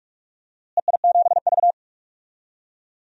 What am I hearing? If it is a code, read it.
Morse code, heard as a keyed tone, about 46 wpm, 700 Hz. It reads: EI7V
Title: DXCC-100 Call Signs - 4 Characters - Repeated 2x - 45wpm